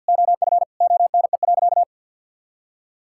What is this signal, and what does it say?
Morse code, heard as a keyed tone, about 50 wpm, 700 Hz. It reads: OF ONE.
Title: Sentences from Top 1000 Words - Review Part 5 - 50wpm